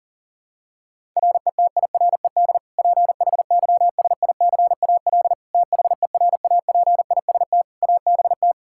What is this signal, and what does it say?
Morse code, heard as a keyed tone, about 40 wpm, 700 Hz. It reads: RETIRED PHYSICAL THERAPIST ABT
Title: Practice QSO - Exchange 3 - More Info - 40wpm